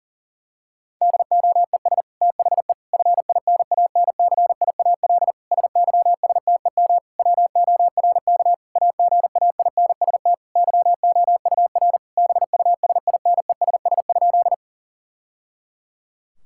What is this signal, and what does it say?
Morse code, heard as a keyed tone, about 37 wpm, 700 Hz. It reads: DOES THE FINANCIAL SYSTEM WORK AGAINST YOUR BUSINESS?
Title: Sentences from Top 1000 Words - Review Part 5 - 40wpm